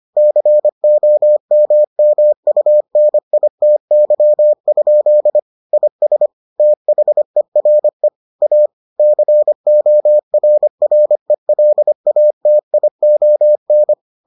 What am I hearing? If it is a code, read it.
Morse code, heard as a keyed tone, about 25 wpm, 600 Hz. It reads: COMMUNITY? IS THERE A CORRELATION